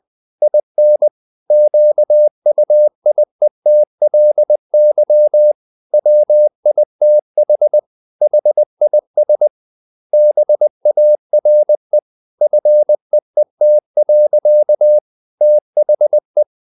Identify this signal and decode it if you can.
Morse code, heard as a keyed tone, about 20 wpm, 600 Hz. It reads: IN QUIETLY WITH HIS BARE FEET. THE
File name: the old man and the sea / the old man and the sea part 1 20wpm